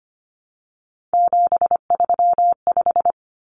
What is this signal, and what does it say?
Morse code, heard as a keyed tone, about 25 wpm, 700 Hz. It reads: 735
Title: Common Cut Numbers - 3-Digit 25wpm